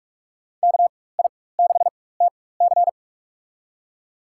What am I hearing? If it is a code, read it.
Morse code, heard as a keyed tone, about 46 wpm, spaced as for 21 wpm, 700 Hz. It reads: KI6TC
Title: USA Call Signs - 5 Characters - 4x Character Spacing - 45wpm